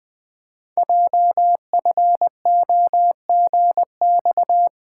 Morse code, heard as a keyed tone, about 20 wpm, 700 Hz. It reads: JFOGX
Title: Code Groups - 5 Letters - 20wpm